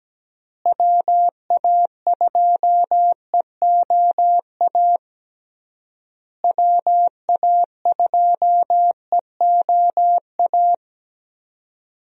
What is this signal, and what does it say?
Morse code, heard as a keyed tone, about 17 wpm, 700 Hz. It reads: WA2EOA WA2EOA
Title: USA Call Signs - 6 Characters - Repeated 3x - 17wpm